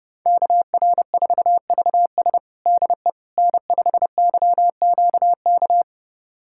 Morse code, heard as a keyed tone, about 30 wpm, 700 Hz. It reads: KR4VS DE N5YQK
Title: USA Callsign DE Callsign - 5 Characters - 30wpm